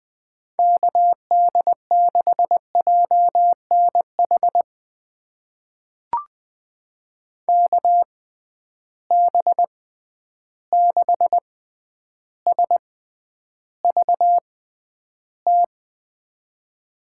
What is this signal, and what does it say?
Morse code, heard as a keyed tone, about 20 wpm, spaced as for 10 wpm, 700 Hz. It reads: KD6JNH KB6SVT
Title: USA Call Signs - 6 Characters - 6x Character Spacing - 20wpm